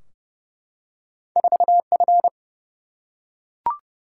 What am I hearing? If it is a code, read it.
Morse code, heard as a keyed tone, about 30 wpm, 700 Hz. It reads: 4F
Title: Single-Letter Single-Number 30wpm